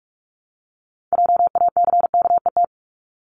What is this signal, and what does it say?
Morse code, heard as a keyed tone, about 46 wpm, 690 Hz. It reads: JACKET
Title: Multi-letter Phonemes 45wpm